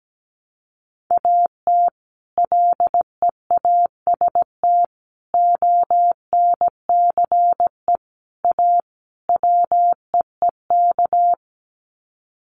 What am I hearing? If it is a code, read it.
Morse code, heard as a keyed tone, about 17 wpm, 700 Hz. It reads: AT LEAST ONCE A WEEK
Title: Sets of 5 Words Encore2 17wpm